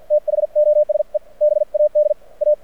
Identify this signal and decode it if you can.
Morse code, heard as a keyed tone, about 56 wpm, spaced as for 46 wpm, 600 Hz. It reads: THOSE BAD A
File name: archived / CRSnet